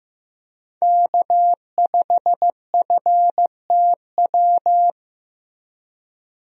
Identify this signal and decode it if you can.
Morse code, heard as a keyed tone, about 15 wpm, 700 Hz. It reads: K5FTW
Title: USA Call Signs - 5 Characters - Repeated 3x - 15wpm